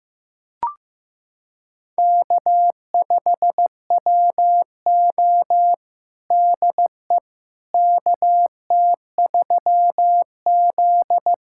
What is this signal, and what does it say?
Morse code, heard as a keyed tone, about 15 wpm, 700 Hz. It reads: K5WO DE KT3Z
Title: USA Callsign DE Callsign - 4 Characters - 15wpm